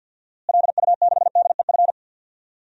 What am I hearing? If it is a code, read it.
Morse code, heard as a keyed tone, about 46 wpm, 700 Hz. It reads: LU6DEF
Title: DXCC-100 Call Signs - 6 Characters - Repeated 2x - 50wpm